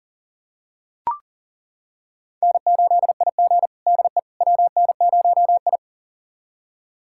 Morse code, heard as a keyed tone, about 40 wpm, 690 Hz. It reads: N8IG DE WN0I
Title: USA Callsign DE Callsign - 4 Characters - 40wpm